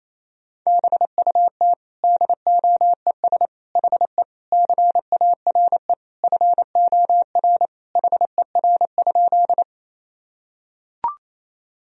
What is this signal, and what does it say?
Morse code, heard as a keyed tone, about 28 wpm, 700 Hz. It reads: BUT DOES HE CARE FOR HER?
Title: Sentences from Top 1000 Words - Review Part 1 - 28wpm